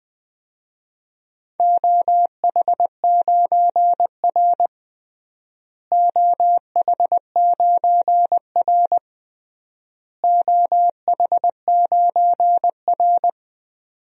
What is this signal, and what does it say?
Morse code, heard as a keyed tone, about 21 wpm, 700 Hz. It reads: OH9R OH9R OH9R
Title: DXCC-100 Call Signs - 4 Characters - Repeated 3x - 20wpm